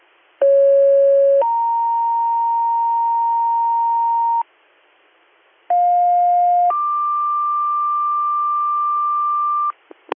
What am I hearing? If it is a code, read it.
Two-tone page, heard as a one-second tone, then a three-second tone, about 550 then 930 Hz, about 710 then 1180 Hz. It is heard from 0.4 to 9.7 s.